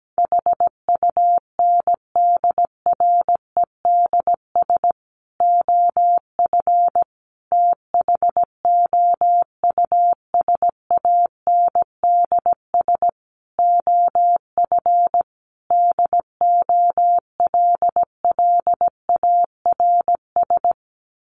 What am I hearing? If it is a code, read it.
Morse code, heard as a keyed tone, about 17 wpm, 700 Hz. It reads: HUNDREDS OF THOUSANDS OF DOLLARS
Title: Sets of 5 Words Encore 17wpm